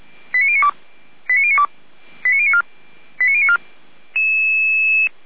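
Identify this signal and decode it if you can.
Five-tone selective call (ZVEI), heard as quick runs of five tones, tones of about 70 ms. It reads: 89092 89092 89094 89094
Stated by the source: ZVEI five-tone selective call received by SDR.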